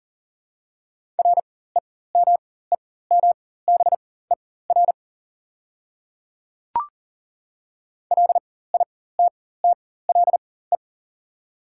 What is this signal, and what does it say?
Morse code, heard as a keyed tone, about 40 wpm, spaced as for 19 wpm, 700 Hz. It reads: REMEMBER LITTLE
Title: 4x Character Spacing - Top 500 Words - 40wpm